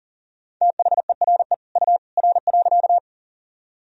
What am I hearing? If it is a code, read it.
Morse code, heard as a keyed tone, about 40 wpm, 700 Hz. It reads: THERE U R.
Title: Sentences from Top 100 Words and Abbreviations 40wpm